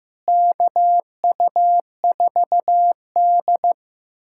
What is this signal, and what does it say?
Morse code, heard as a keyed tone, about 15 wpm, 700 Hz. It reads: KU4D
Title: USA Callsign DE Callsign - 4 Characters - 15wpm